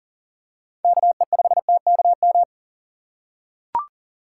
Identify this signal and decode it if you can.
Morse code, heard as a keyed tone, about 40 wpm, 700 Hz. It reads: KE5TKM
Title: USA Call Signs - 6 Characters - 2x Character Spacing - 40wpm